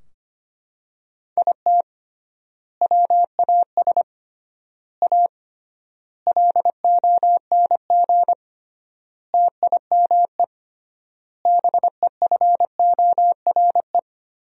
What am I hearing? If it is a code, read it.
Morse code, heard as a keyed tone, about 25 wpm, 700 Hz. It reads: IT WAS A LONG TIME BEFORE
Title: 3x Word-Spacing - Sentences from Top 500 Words - 25wpm